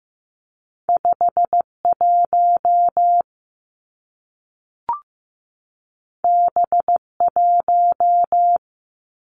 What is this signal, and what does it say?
Morse code, heard as a keyed tone, about 15 wpm, 700 Hz. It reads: 5 1 B 1
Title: Two Challenging Letter-Number Combinations 15wpm